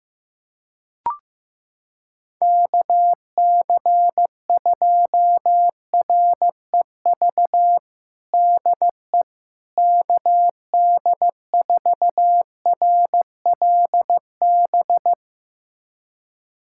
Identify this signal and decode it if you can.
Morse code, heard as a keyed tone, about 15 wpm, 700 Hz. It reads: KC2REV DE KD4RLB
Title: USA Callsign DE Callsign - 6 Characters - 15wpm